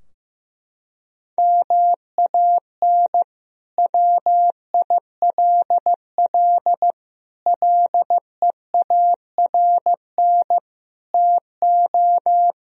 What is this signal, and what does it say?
Morse code, heard as a keyed tone, about 16 wpm, 700 Hz. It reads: MAN WILL LEARN TO
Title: Sentences from Top 1000 Words - Review Part 2 - 15wpm